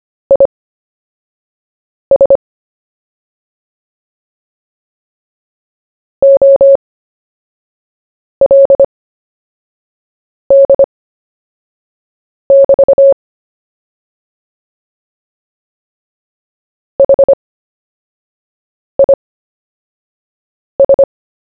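Morse code, heard as a keyed tone, about 25 wpm, spaced as for 5 wpm, 550 Hz. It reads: IS OLD<BT> HIS
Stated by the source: CW story-telling practice, Short Story 1 sent at 5 wpm, by VK4JE, Queensland, Australia.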